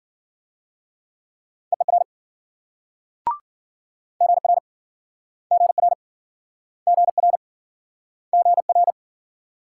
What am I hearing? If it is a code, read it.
Morse code, heard as a keyed tone, about 56 wpm, 700 Hz. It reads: EER GR GR GR GR